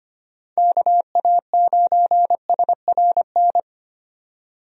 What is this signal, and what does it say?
Morse code, heard as a keyed tone, about 25 wpm, 700 Hz. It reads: KA9SRN
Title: USA Call Signs - 6 Characters - 25wpm